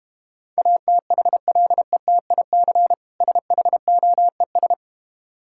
Morse code, heard as a keyed tone, about 32 wpm, 700 Hz. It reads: ATHLETIC SHOES